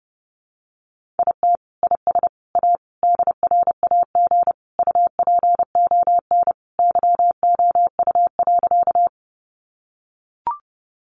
Morse code, heard as a keyed tone, about 30 wpm, 700 Hz. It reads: IT IS A DRAG UPON YOU.
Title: Sentences from Top 1500 Words 30wpm